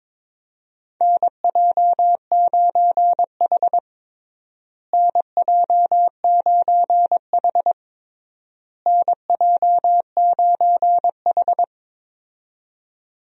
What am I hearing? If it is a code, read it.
Morse code, heard as a keyed tone, about 22 wpm, 700 Hz. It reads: NJ9H NJ9H NJ9H
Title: USA Call Signs - 4 Characters - Repeated 3x - 22wpm